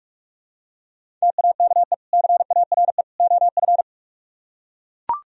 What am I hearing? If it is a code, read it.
Morse code, heard as a keyed tone, about 44 wpm, 700 Hz. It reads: TAKE CARE OF